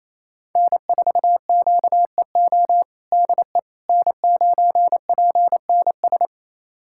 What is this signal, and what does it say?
Morse code, heard as a keyed tone, about 28 wpm, 700 Hz. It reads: N4QEO DE N9PNS
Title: USA Callsign DE Callsign - 5 Characters - 28wpm